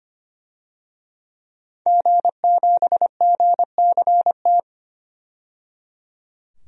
Morse code, heard as a keyed tone, about 25 wpm, 700 Hz. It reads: G7GCT
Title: DXCC-100 Call Signs - 5 Characters - 25wpm